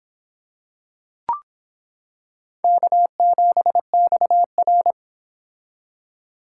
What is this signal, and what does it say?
Morse code, heard as a keyed tone, about 25 wpm, 700 Hz. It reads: K7XR